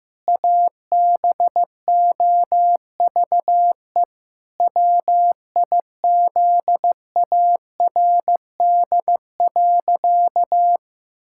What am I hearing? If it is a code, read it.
Morse code, heard as a keyed tone, about 15 wpm, 700 Hz. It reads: ABOVE WIZARD.
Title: Pangram Challenge v2 - 15wpm